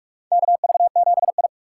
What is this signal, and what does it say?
Morse code, heard as a keyed tone, about 46 wpm, 700 Hz. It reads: KV7I
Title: USA Call Signs - 3x Character Spacing - 45wpm